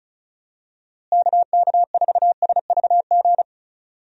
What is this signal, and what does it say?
Morse code, heard as a keyed tone, about 34 wpm, 700 Hz. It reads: KK4SVG
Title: USA Call Signs - 6 Characters - Repeated 3x - 35wpm